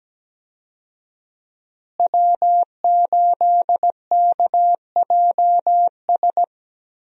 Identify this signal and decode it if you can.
Morse code, heard as a keyed tone, about 17 wpm, 700 Hz. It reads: W8KJS
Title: USA Call Signs - 5 Characters - Repeated 3x - 17wpm